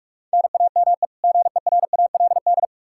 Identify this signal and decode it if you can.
Morse code, heard as a keyed tone, about 46 wpm, 700 Hz. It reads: NAME GERALD
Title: Practice QSO - Exchange 1 - Beginning - 45wpm